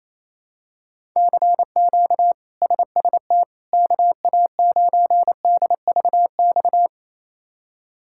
Morse code, heard as a keyed tone, about 28 wpm, 700 Hz. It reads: CQ SST KA9DVX
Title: K1USN SST QSO Practice - 28wpm